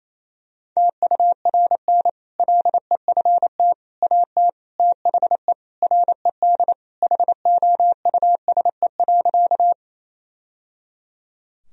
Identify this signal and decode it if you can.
Morse code, heard as a keyed tone, about 28 wpm, 700 Hz. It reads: TURN LEFT AT THE RED HOUSE.
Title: Sentences from Top 1000 Words - Review Part 2 - 28wpm